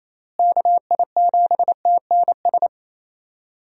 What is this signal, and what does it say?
Morse code, heard as a keyed tone, about 28 wpm, 700 Hz. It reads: KI7TNS
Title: USA Call Signs - 6 Characters - Repeated 3x - 28wpm